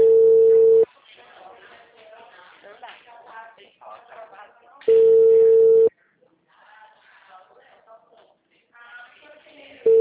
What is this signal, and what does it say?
Ringback tone (China, 450 Hz), with speech.